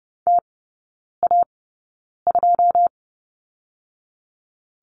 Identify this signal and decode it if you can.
Morse code, heard as a keyed tone, about 30 wpm, spaced as for 9 wpm, 700 Hz. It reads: TA2
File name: course / 30:10wpm / 087 Lesson 15e - Number 5 - Three Characters 30wpm